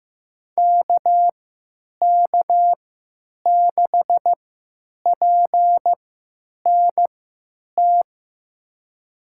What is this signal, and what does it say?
Morse code, heard as a keyed tone, about 15 wpm, spaced as for 9 wpm, 700 Hz. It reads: KK6PNT
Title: USA Call Signs - 6 Characters - 3x Character Spacing - 15wpm